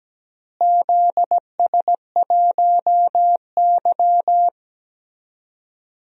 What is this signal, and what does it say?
Morse code, heard as a keyed tone, about 17 wpm, 700 Hz. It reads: ZS1Y
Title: DXCC-100 Call Signs - 4 Characters - Repeated 2x - 17wpm